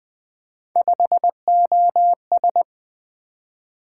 Morse code, heard as a keyed tone, about 20 wpm, 700 Hz. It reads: 5OS
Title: Three Letter-Number Combinations - 20wpm